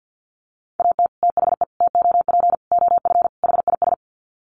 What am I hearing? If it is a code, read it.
Morse code, heard as a keyed tone, about 50 wpm, 700 Hz. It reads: AT THE TOP OF HIS
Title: Sets of 5 Words Encore 50wpm